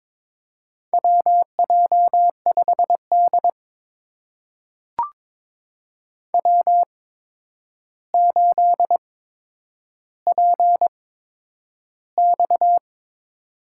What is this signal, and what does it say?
Morse code, heard as a keyed tone, about 22 wpm, spaced as for 9 wpm, 700 Hz. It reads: WJ5D W8PX
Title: USA Call Signs - 4 Characters - 8x Character Spacing - 22wpm